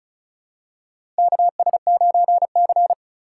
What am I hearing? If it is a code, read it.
Morse code, heard as a keyed tone, about 36 wpm, 700 Hz. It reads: KS9C
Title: USA Call Signs - 4 Characters - 2x Character Spacing - 35wpm